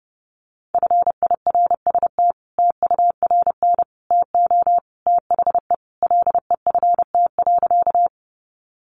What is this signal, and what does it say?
Morse code, heard as a keyed tone, about 29 wpm, 700 Hz. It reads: FIRST TURN TO THE LEFT.
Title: Sentences from Top 300 Words 30wpm